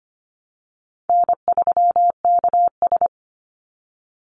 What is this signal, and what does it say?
Morse code, heard as a keyed tone, about 25 wpm, 700 Hz. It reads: N3KS
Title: Call Signs Repeated 3x - 25wpm